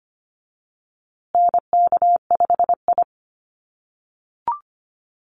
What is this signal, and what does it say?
Morse code, heard as a keyed tone, about 25 wpm, 700 Hz. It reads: NK5I
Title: USA Call Signs Repeated 2x 25wpm